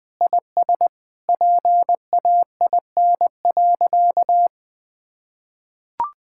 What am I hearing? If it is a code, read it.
Morse code, heard as a keyed tone, about 21 wpm, 700 Hz. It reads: IS PAIN.
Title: Sentences from Top 1600 Words 20wpm